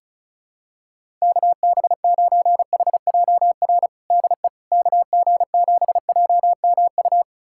Morse code, heard as a keyed tone, about 36 wpm, 700 Hz. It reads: KB9HJR DE KG7JMU
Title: USA Callsign DE Callsign - 6 Characters - 35wpm